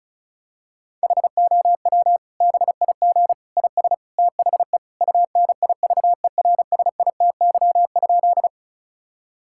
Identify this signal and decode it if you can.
Morse code, heard as a keyed tone, about 36 wpm, 700 Hz. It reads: HOW BIG IS THE UNIVERSITY?
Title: Sentences from Top 1000 Words - Review Part 5 - 35wpm